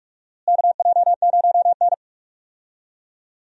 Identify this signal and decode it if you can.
Morse code, heard as a keyed tone, about 42 wpm, 700 Hz. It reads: KJ0N
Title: USA Call Signs - 4 Characters - Repeated 3x - 45wpm